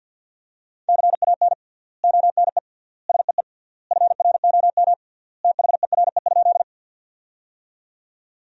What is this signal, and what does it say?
Morse code, heard as a keyed tone, about 47 wpm, 700 Hz. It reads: CAN ONE SEE FROM THERE?
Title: 3x Word-Spacing - Sentences from Top 100 Words - 50wpm